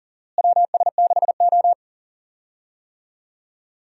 Morse code, heard as a keyed tone, about 40 wpm, 700 Hz. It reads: WS6O
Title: USA Call Signs - 4 Characters - Repeated 3x - 40wpm